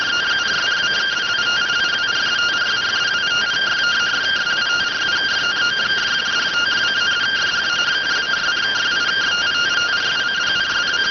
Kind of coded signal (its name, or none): two-tone teleprinter signal